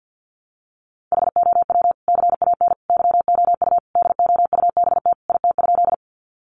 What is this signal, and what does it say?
Morse code, heard as a keyed tone, about 50 wpm, 700 Hz. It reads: HOW CAN YOU DOUBT IT?
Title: Sentences from Top 1500 Words 50wpm